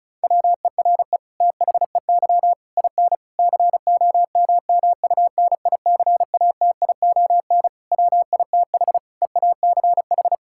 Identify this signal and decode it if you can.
Morse code, heard as a keyed tone, about 34 wpm, 700 Hz. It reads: WERE THEY IN COMMUNICATION WITH EACH
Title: Sentences from Top 1000 Words - Review Part 6 - 35wpm